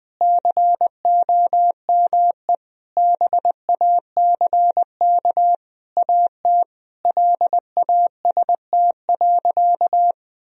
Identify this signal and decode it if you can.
Morse code, heard as a keyed tone, about 20 wpm, 700 Hz. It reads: COME BACK AT LAST.
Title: Sentences from Top 1000 Words - Review Part 2 - 20wpm